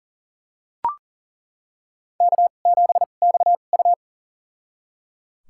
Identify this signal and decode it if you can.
Morse code, heard as a keyed tone, about 37 wpm, spaced as for 28 wpm, 700 Hz. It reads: K7XU
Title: USA Call Signs - 4 Characters - 2x Character Spacing - 40wpm